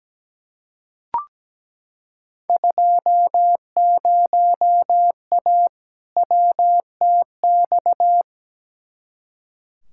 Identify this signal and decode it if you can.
Morse code, heard as a keyed tone, about 17 wpm, 700 Hz. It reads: 20A WTX
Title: Field Day Exchanges Encore - 17wpm